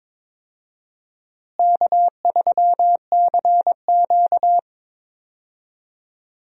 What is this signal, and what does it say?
Morse code, heard as a keyed tone, about 22 wpm, 700 Hz. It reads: K3CQ
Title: USA Call Signs - 4 Characters - Repeated 3x - 22wpm